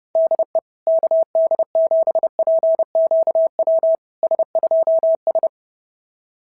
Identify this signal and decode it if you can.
Morse code, heard as a keyed tone, about 28 wpm, 650 Hz. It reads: DE KD7PQW S2S
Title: SOTA QSO Practice 30wpm